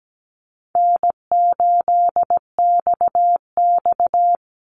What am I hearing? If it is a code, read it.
Morse code, heard as a keyed tone, about 17 wpm, 700 Hz. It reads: N8XX
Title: Easy Call Signs - 17wpm